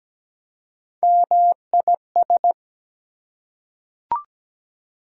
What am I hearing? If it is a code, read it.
Morse code, heard as a keyed tone, about 17 wpm, 700 Hz. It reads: MIS
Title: Prefixes and Suffixes 17wpm